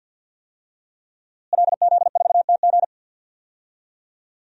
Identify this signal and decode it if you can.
Morse code, heard as a keyed tone, about 52 wpm, 700 Hz. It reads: LZ4TG
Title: DXCC-100 Call Signs - 5 Characters - Repeated 2x - 50wpm